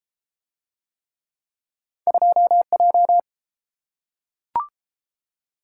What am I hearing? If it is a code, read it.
Morse code, heard as a keyed tone, about 31 wpm, 700 Hz. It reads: 2J